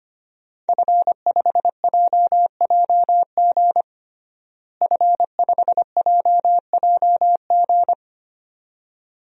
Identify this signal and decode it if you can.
Morse code, heard as a keyed tone, about 25 wpm, 700 Hz. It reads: F5JJG F5JJG
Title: DXCC-100 Call Signs - 5 Characters - Repeated 2x - 25wpm